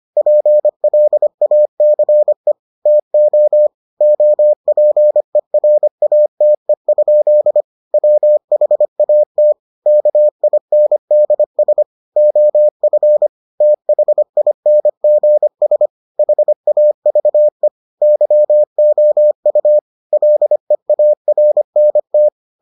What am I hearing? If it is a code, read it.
Morse code, heard as a keyed tone, about 25 wpm, 600 Hz. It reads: PLACE TO OPERATE? WHAT KINDS OF THINGS HAVE YOU LEARNT